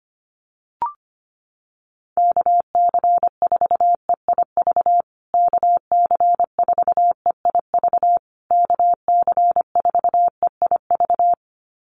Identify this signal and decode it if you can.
Morse code, heard as a keyed tone, about 25 wpm, 700 Hz. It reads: KC4EIV KC4EIV KC4EIV
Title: USA Call Signs Repeated 3x 25wpm